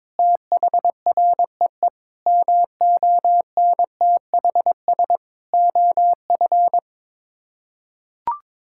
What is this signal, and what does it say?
Morse code, heard as a keyed tone, about 22 wpm, 700 Hz. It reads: THREE MONTHS OF
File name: practice / 22wpm / Sets of 6 Words - 22wpm